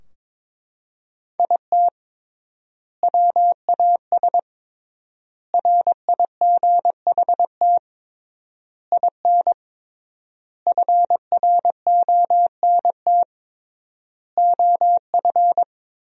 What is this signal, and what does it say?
Morse code, heard as a keyed tone, about 22 wpm, 700 Hz. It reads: IT WAS RIGHT IN FRONT OF
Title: 3x Word-Spacing - Sentences from Top 500 Words - 22wpm